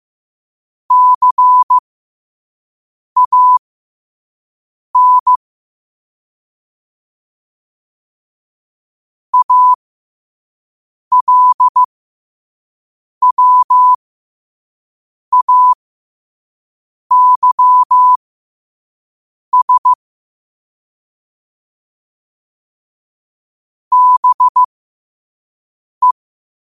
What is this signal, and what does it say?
Morse code, heard as a keyed tone, about 15 wpm, spaced as for 5 wpm, 1000 Hz. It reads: CAN ALWAYS BE
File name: qotdfiles / 5wpm